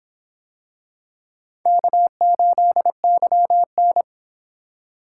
Morse code, heard as a keyed tone, about 26 wpm, 700 Hz. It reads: K8YN